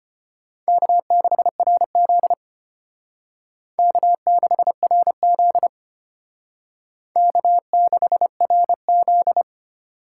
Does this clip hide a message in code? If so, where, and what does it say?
Yes: Morse code from 0.7 to 9.4 s, heard as a keyed tone, about 33 wpm, 700 Hz. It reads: K6RZ K6RZ K6RZ